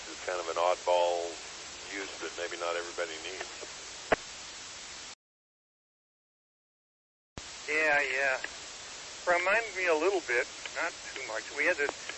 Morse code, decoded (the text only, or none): none